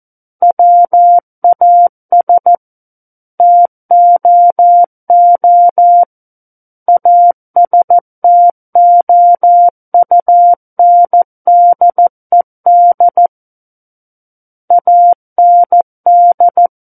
Morse code, heard as a keyed tone, about 14 wpm, 700 Hz. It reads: WAS TOO ASTOUNDED AND
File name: Book One / 05-Chapter 5 - 14 WPM